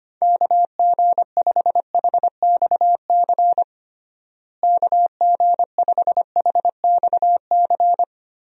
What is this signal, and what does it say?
Morse code, heard as a keyed tone, about 25 wpm, 700 Hz. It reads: KG5HXC KG5HXC
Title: USA Call Signs - 6 Characters - Repeated 3x - 25wpm